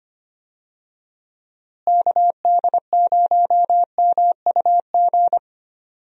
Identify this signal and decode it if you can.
Morse code, heard as a keyed tone, about 25 wpm, 700 Hz. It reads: KD0MUG
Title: USA Call Signs - 6 Characters - Repeated 3x - 25wpm